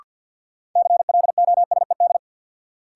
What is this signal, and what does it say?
Morse code, heard as a keyed tone, about 50 wpm, 700 Hz. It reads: CLOSED
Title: Sentences from Top 1400 Words - New Word Drill - 50wpm